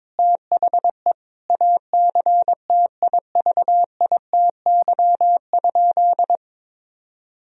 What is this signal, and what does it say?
Morse code, heard as a keyed tone, about 22 wpm, 700 Hz. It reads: THE ACTIVITY?
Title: Sentences from Top 1000 Words - Review Part 4 - 22wpm